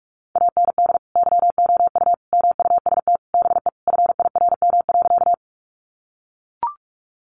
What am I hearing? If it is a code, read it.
Morse code, heard as a keyed tone, about 44 wpm, 700 Hz. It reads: AND YOU MUST BE FIRM.
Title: Sentences from Top 1400 Words 45wpm